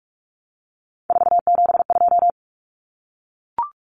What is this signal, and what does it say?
Morse code, heard as a keyed tone, about 44 wpm, 700 Hz. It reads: 472
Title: Numbers - 3-Digit 45wpm